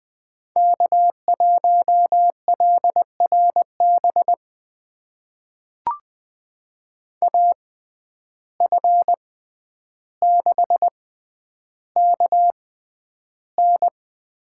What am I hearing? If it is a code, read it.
Morse code, heard as a keyed tone, about 20 wpm, spaced as for 10 wpm, 700 Hz. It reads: K1LRB AF6KN
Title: USA Call Signs - 5 Characters - 6x Character Spacing - 20wpm